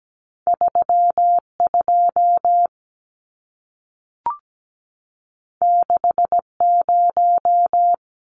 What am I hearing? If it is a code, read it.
Morse code, heard as a keyed tone, about 17 wpm, 700 Hz. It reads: 32 60
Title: Numbers - 2-Digit 17wpm